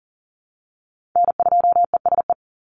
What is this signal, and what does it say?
Morse code, heard as a keyed tone, about 41 wpm, 700 Hz. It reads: N2ESE
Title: NA SOTA Chaser Callsigns 40wpm